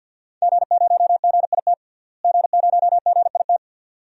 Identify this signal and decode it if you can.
Morse code, heard as a keyed tone, about 52 wpm, 700 Hz. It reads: G0GIT G0GIT
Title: DXCC-100 Call Signs - 5 Characters - Repeated 2x - 50wpm